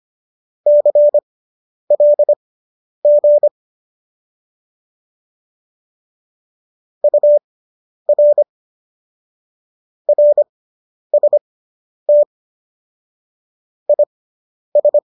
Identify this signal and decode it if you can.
Morse code, heard as a keyed tone, about 25 wpm, spaced as for 10 wpm, 590 Hz. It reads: CLG UR RST IS